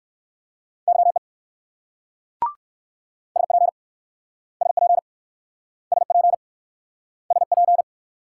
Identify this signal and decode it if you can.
Morse code, heard as a keyed tone, about 56 wpm, 700 Hz. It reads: YE SP SP SP SP